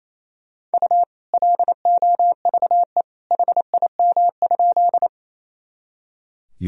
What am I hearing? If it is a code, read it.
Morse code, heard as a keyed tone, about 28 wpm, 700 Hz. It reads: U LOVE HIM?
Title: Sentences from Top 400 Words and Abbreviations 28wpm